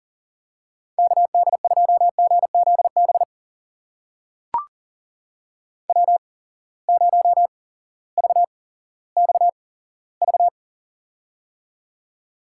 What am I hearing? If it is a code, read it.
Morse code, heard as a keyed tone, about 40 wpm, spaced as for 18 wpm, 700 Hz. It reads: KD2GZB W0VXV
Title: USA Call Signs - 8x Character Spacing - 40wpm